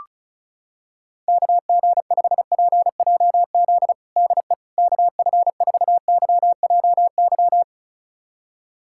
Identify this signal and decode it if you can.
Morse code, heard as a keyed tone, about 36 wpm, 700 Hz. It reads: KG5PJZ DE KF4YJY
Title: USA Callsign DE Callsign - 6 Characters - 35wpm